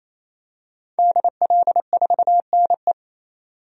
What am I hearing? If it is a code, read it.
Morse code, heard as a keyed tone, about 28 wpm, 700 Hz. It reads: DL4NE
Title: DXCC-100 Call Signs - 5 Characters - 28wpm